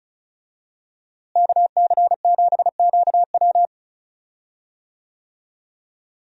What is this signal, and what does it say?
Morse code, heard as a keyed tone, about 36 wpm, 700 Hz. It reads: KC7QW
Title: USA Call Signs - 5 Characters - Repeated 3x - 35wpm